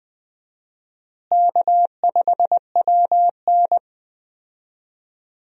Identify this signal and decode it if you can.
Morse code, heard as a keyed tone, about 20 wpm, 700 Hz. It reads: K5WN
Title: USA Call Signs - 4 Characters - Repeated 3x - 20wpm